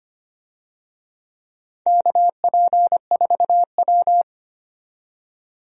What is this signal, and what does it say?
Morse code, heard as a keyed tone, about 25 wpm, 700 Hz. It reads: KP4W
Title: DXCC-100 Call Signs - 4 Characters - 25wpm